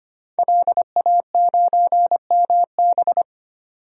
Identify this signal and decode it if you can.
Morse code, heard as a keyed tone, about 25 wpm, 700 Hz. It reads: LA9MB
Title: DXCC-100 Call Signs - 5 Characters - 25wpm